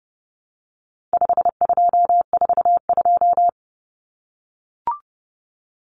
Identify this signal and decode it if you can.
Morse code, heard as a keyed tone, about 30 wpm, 700 Hz. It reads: 5242
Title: Numbers - 4-Digit 30wpm